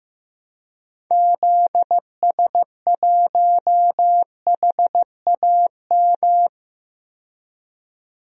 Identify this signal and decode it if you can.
Morse code, heard as a keyed tone, about 15 wpm, 700 Hz. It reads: ZS1HAM
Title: DXCC-100 Call Signs - 6 Characters - 15wpm